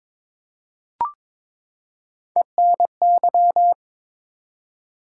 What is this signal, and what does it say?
Morse code, heard as a keyed tone, about 21 wpm, 700 Hz. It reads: ENY
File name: ARRL Sections v2 - 22wpm